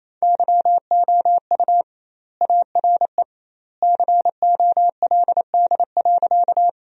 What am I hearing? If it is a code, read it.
Morse code, heard as a keyed tone, about 28 wpm, 700 Hz. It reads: YOU ARE COLD.
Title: 2x Word-Spacing - Sentences from Top 400 Words - 28wpm